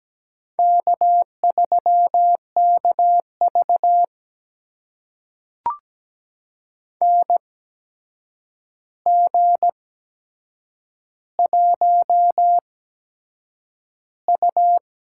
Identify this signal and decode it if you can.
Morse code, heard as a keyed tone, about 17 wpm, spaced as for 7 wpm, 700 Hz. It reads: K3KV NG1U
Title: USA Call Signs - 4 Characters - 8x Character Spacing - 17wpm